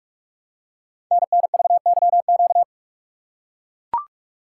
Morse code, heard as a keyed tone, about 44 wpm, 700 Hz. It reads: NNVYQ
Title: Code Groups - 5 Letters - 45wpm